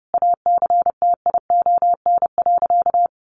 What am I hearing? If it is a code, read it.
Morse code, heard as a keyed tone, about 30 wpm, 700 Hz. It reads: ACTION.
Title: Sentences from Top 700 Words 30wpm